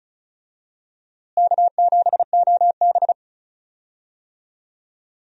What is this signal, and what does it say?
Morse code, heard as a keyed tone, about 36 wpm, 700 Hz. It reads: K7OB
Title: USA Call Signs - 4 Characters - Repeated 2x - 35wpm